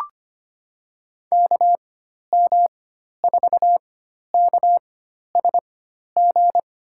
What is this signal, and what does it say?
Morse code, heard as a keyed tone, about 25 wpm, spaced as for 12 wpm, 700 Hz. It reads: KM4KSG
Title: USA Call Signs - 6 Characters - 4x Character Spacing - 25wpm